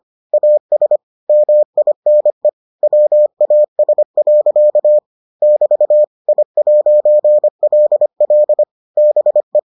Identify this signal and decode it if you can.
Morse code, heard as a keyed tone, about 25 wpm, 600 Hz. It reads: AS MINE WAS. <BT> I'LL BE